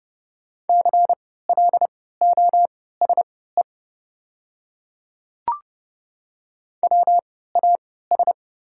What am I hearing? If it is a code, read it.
Morse code, heard as a keyed tone, about 31 wpm, spaced as for 17 wpm, 690 Hz. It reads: CLOSE WAS
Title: 3x Character Spacing - Top 300 Words - 30wpm